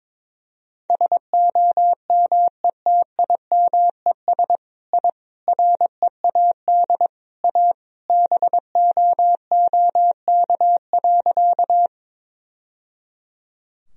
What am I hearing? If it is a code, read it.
Morse code, heard as a keyed tone, about 22 wpm, 700 Hz. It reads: SOMETIMES I READ A BOOK.
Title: Sentences from Top 1000 Words - Review Part 3 - 22wpm